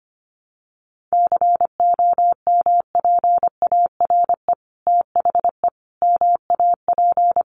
Morse code, heard as a keyed tone, about 25 wpm, 700 Hz. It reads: COMPARE THE MAP
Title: Sentences from Top 1300 Words 25wpm